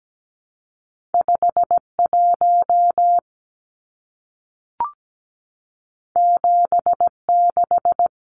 Morse code, heard as a keyed tone, about 17 wpm, 700 Hz. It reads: 51 76
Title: Numbers - 2-Digit 17wpm